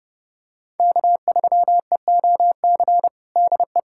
Morse code, heard as a keyed tone, about 32 wpm, 700 Hz. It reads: K3EOC DE
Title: USA Callsign DE Callsign - 5 Characters - 30wpm